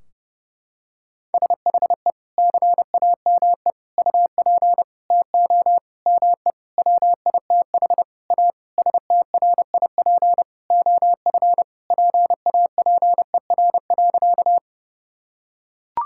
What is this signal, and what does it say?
Morse code, heard as a keyed tone, about 30 wpm, 700 Hz. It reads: SHE CAME UP TO ME WITH A STRIP OF PAPER.
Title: Sentences from Top 1600 Words 30wpm